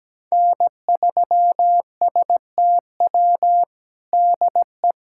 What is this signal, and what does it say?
Morse code, heard as a keyed tone, about 17 wpm, 700 Hz. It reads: N3STW DE
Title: USA Callsign DE Callsign - 5 Characters - 17wpm